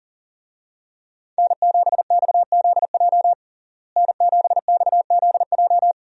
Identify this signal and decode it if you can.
Morse code, heard as a keyed tone, about 40 wpm, 700 Hz. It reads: N7XZJ N7XZJ
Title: USA Call Signs - 5 Characters - Repeated 3x - 40wpm